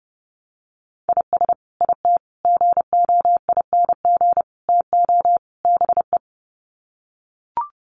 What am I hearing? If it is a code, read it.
Morse code, heard as a keyed tone, about 30 wpm, 700 Hz. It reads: IS IT GOING TO BE
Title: Sets of 5 Words Encore2 30wpm